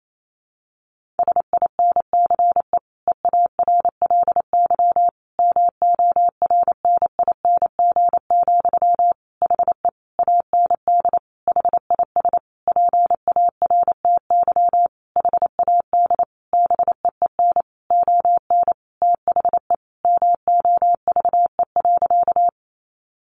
Morse code, heard as a keyed tone, about 28 wpm, 700 Hz. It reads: SINCE EARLY MORNING, HE AND HIS PARTY HAD BEEN ON THE MOVE.
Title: Sentences from Top 1400 Words 28wpm